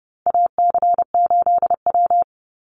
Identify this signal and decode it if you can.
Morse code, heard as a keyed tone, about 30 wpm, 700 Hz. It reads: AC8W
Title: Call Signs Repeated 3x - 30wpm